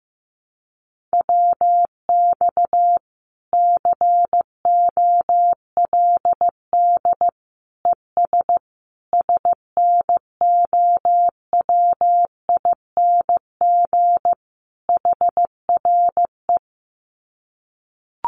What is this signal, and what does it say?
Morse code, heard as a keyed tone, about 15 wpm, 700 Hz. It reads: WX COLD ES SNOWING HRE
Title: Common QSO Phrases 15wpm